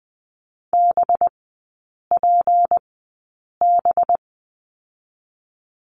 Morse code, heard as a keyed tone, about 20 wpm, spaced as for 8 wpm, 700 Hz. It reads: BPB
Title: 205 Lesson 33e - Letter j - Three Characters 20wpm